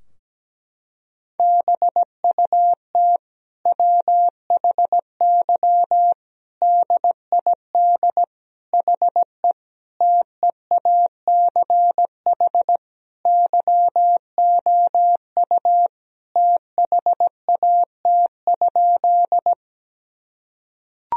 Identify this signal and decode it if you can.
Morse code, heard as a keyed tone, about 17 wpm, 700 Hz. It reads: BUT WHY DID HE TEACH YOU THAT?
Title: Sentences from Top 1000 Words - Review Part 2 - 17wpm